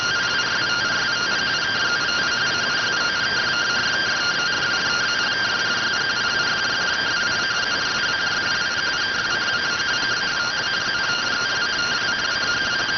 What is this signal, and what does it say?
Two-tone teleprinter signal, about 170 Hz shift.